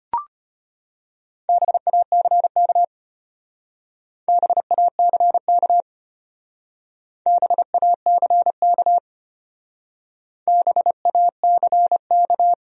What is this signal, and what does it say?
Morse code, heard as a keyed tone, about 32 wpm, 700 Hz. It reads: BACK BACK BACK BACK